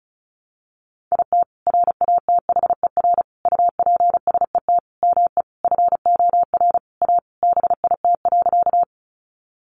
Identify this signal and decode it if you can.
Morse code, heard as a keyed tone, about 34 wpm, 700 Hz. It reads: IT RATHER UPSET ME FOR A BIT.
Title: Sentences from Top 900 Words 35wpm